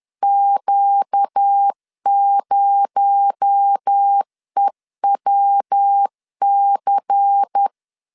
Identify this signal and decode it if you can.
Morse code, heard as a keyed tone, about 10 wpm, 800 Hz. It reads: Q0EWC
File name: CW / DET / 1 9 1500C S B 10wpm 4439